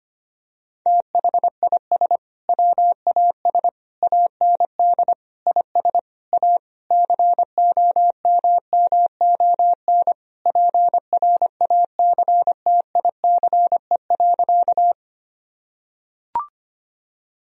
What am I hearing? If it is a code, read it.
Morse code, heard as a keyed tone, about 25 wpm, 700 Hz. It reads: THIS WAS AND IS A COMMON PRACTICE.
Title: Sentences from Top 1000 Words - Review Part 4 - 25wpm